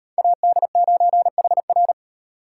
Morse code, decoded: AD9HR

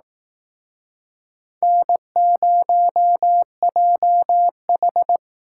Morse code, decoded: N0JH

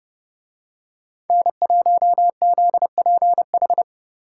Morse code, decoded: N1ZPH

